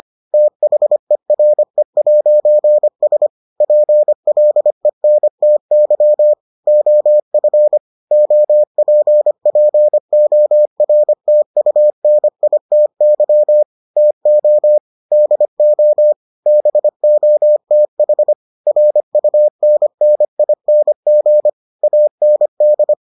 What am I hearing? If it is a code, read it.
Morse code, heard as a keyed tone, about 25 wpm, 600 Hz. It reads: THERE'S PLENTY OF OPPORTUNITY TO DO BOTH RUNNING AND